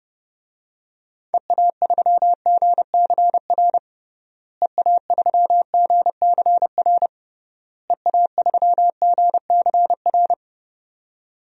Morse code, heard as a keyed tone, about 31 wpm, 700 Hz. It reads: EA3GCR EA3GCR EA3GCR
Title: DXCC-100 Call Signs - 6 Characters - Repeated 3x - 30wpm